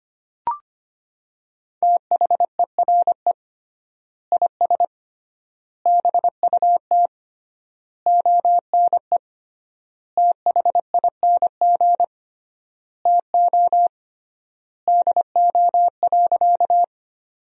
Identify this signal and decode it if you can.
Morse code, heard as a keyed tone, about 25 wpm, 700 Hz. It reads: THERE IS BUT ONE THING TO DO.
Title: 3x Word-Spacing - Sentences from Top 100 Words - 25wpm